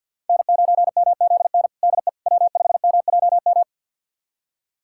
Morse code, heard as a keyed tone, about 46 wpm, 700 Hz. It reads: N9MZN DE W5MJM
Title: USA Callsign DE Callsign - 5 Characters - 50wpm